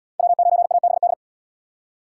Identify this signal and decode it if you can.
Morse code, heard as a keyed tone, about 66 wpm, 700 Hz. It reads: W9ABD